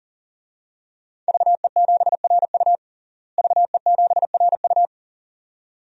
Morse code, heard as a keyed tone, about 40 wpm, 700 Hz. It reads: VE7RU VE7RU
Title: DXCC-100 Call Signs - 5 Characters - Repeated 2x - 40wpm